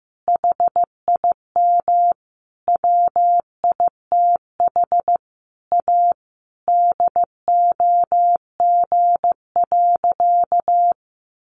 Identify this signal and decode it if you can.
Morse code, heard as a keyed tone, about 15 wpm, 700 Hz. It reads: HIM WITH A DOG.
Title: Additional Easy Phrases 15wpm